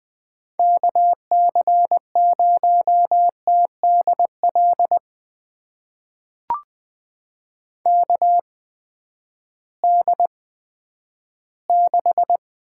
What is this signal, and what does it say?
Morse code, heard as a keyed tone, about 20 wpm, spaced as for 11 wpm, 700 Hz. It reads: KC0TDL KD6
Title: USA Call Signs - 8x Character Spacing - 20wpm